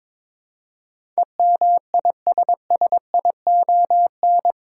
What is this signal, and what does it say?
Morse code, heard as a keyed tone, about 22 wpm, 700 Hz. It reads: EMISSION
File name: practice / 22wpm / Eight Letter Words - 22wpm